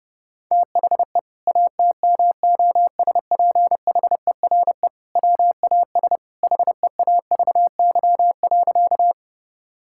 Morse code, heard as a keyed tone, about 30 wpm, 700 Hz. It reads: THE ATMOSPHERE WAS HEAVY.
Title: Sentences from Top 1600 Words 30wpm